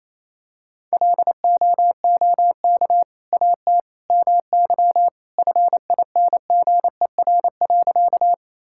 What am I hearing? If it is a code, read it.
Morse code, heard as a keyed tone, about 28 wpm, 700 Hz. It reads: LOOK AT MY FINGER.
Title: Sentences from Top 1600 Words 28wpm